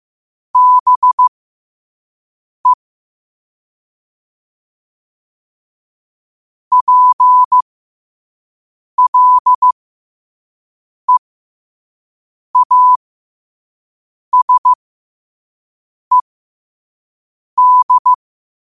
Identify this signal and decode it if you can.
Morse code, heard as a keyed tone, about 15 wpm, spaced as for 5 wpm, 1000 Hz. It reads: BE PLEASED